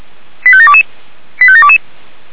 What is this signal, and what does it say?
Five-tone selective call (ZVEI), heard as quick runs of five tones, tones of about 70 ms. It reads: 85722 85722
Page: SDR/ZVEI